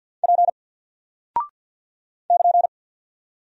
Morse code, heard as a keyed tone, about 48 wpm, 700 Hz. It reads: P <KN>
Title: Single Character (Letter:Number:Punctuation:Prosign) Alt 50wpm